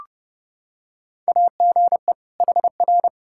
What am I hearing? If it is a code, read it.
Morse code, heard as a keyed tone, about 30 wpm, 700 Hz. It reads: AGE HR
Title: K1USN SST QSO Practice 30wpm